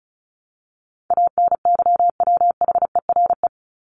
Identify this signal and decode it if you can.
Morse code, heard as a keyed tone, about 34 wpm, 700 Hz. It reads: ANYWHERE
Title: Top 1251-1500 Words 35wpm